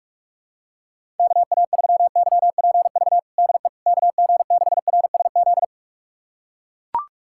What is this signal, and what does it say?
Morse code, heard as a keyed tone, about 42 wpm, 700 Hz. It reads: KA3YPV DE KG6RSZ
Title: USA Callsign DE Callsign 45wpm